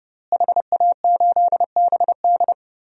Morse code, heard as a keyed tone, about 32 wpm, 700 Hz. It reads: HA8BD
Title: DXCC-100 Call Signs - 5 Characters - Repeated 2x - 30wpm